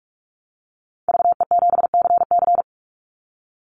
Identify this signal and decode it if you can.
Morse code, heard as a keyed tone, about 44 wpm, 700 Hz. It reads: VE7CC
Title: Call Signs Repeated 2x - 45wpm